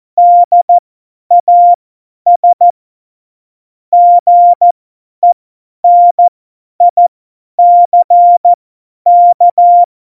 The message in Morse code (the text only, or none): DAS GENICK